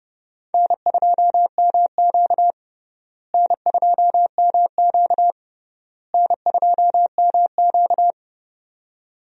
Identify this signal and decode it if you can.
Morse code, heard as a keyed tone, about 30 wpm, 700 Hz. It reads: N2MQ N2MQ N2MQ
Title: USA Call Signs - 4 Characters - Repeated 3x - 30wpm